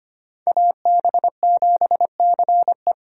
Morse code, heard as a keyed tone, about 32 wpm, 700 Hz. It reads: AB7CE